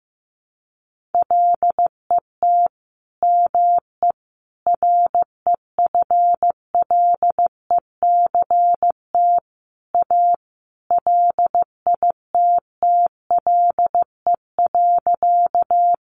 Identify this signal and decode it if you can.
Morse code, heard as a keyed tone, about 15 wpm, 700 Hz. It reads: LET ME REFLECT A LITTLE.
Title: Sentences from Top 1500 Words 15wpm